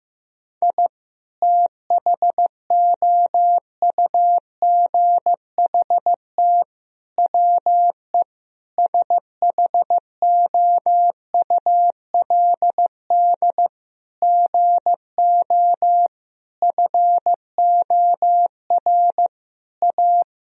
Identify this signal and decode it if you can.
Morse code, heard as a keyed tone, about 16 wpm, 700 Hz. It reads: I THOUGHT WE SHOULD GO FOR A
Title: Sentences from Top 1000 Words - Review Part 2 - 15wpm